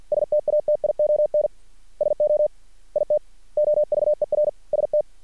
Morse code, heard as a keyed tone, about 50 wpm, 600 Hz. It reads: STATION SO IT OVER ST
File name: archived / CRSnet